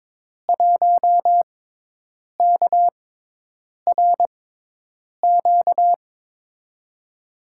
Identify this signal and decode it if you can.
Morse code, heard as a keyed tone, about 22 wpm, spaced as for 8 wpm, 700 Hz. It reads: 1KRQ
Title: USA Call Signs - 5 Characters - 6x Character Spacing - 22wpm